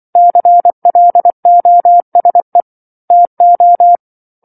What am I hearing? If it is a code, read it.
Morse code, heard as a keyed tone, about 24 wpm, 700 Hz. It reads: CLOSE TO
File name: 07-Chapter-07 24 WPM